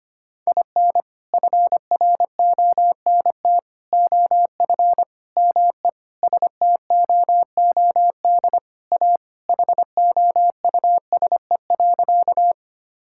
Morse code, heard as a keyed tone, about 25 wpm, 700 Hz. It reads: IN FRONT OF ME STOOD A HOUSE.
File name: practice / 25wpm / Sentences with 4-Word Phrases - 25wpm